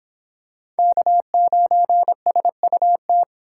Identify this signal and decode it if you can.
Morse code, heard as a keyed tone, about 25 wpm, 700 Hz. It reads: K9SUT